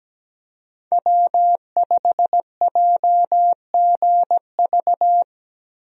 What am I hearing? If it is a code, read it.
Morse code, heard as a keyed tone, about 17 wpm, 700 Hz. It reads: W5JGV
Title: USA Call Signs - 5 Characters - Repeated 2x - 17wpm